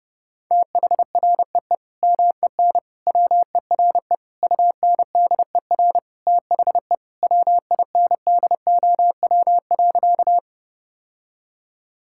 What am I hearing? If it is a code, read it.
Morse code, heard as a keyed tone, about 30 wpm, 700 Hz. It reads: THREE MEN WERE UNDER THE WINDOW.
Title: Sentences from Top 1600 Words 30wpm